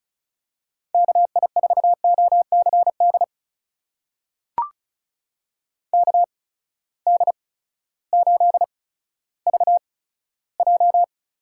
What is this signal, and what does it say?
Morse code, heard as a keyed tone, about 36 wpm, spaced as for 16 wpm, 700 Hz. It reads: KI4OCD KD8VJ
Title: USA Call Signs - 8x Character Spacing - 35wpm